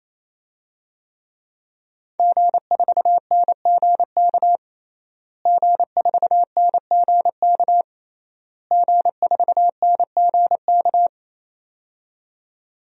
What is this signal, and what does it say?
Morse code, heard as a keyed tone, about 28 wpm, 700 Hz. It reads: G4NGK G4NGK G4NGK
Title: DXCC-100 Call Signs - 5 Characters - Repeated 3x - 28wpm